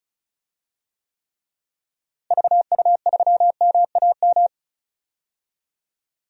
Morse code, heard as a keyed tone, about 36 wpm, 700 Hz. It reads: VU3MAM
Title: DXCC-100 Call Signs - 6 Characters - Repeated 2x - 35wpm